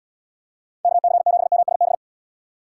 Morse code, heard as a keyed tone, about 75 wpm, 700 Hz. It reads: KC7MSB